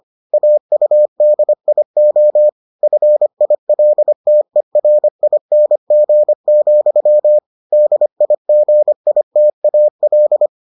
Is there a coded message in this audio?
Yes: Morse code from 0.3 to 10.5 s, heard as a keyed tone, about 25 wpm, 600 Hz. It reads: AUDIO FILTERING, DIGITAL